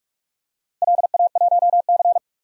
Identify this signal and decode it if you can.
Morse code, heard as a keyed tone, about 44 wpm, 700 Hz. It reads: LA1C
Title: DXCC-100 Call Signs - 4 Characters - 45wpm